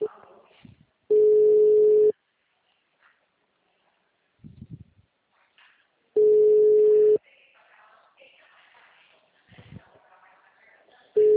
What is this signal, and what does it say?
Ringback tone (Europe, 425 Hz).